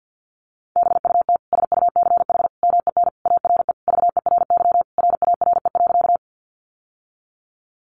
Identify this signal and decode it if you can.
Morse code, heard as a keyed tone, about 50 wpm, 700 Hz. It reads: BUT SUCH MEN ARE VERY RARE.
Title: Sentences from Top 1100 Words 50wpm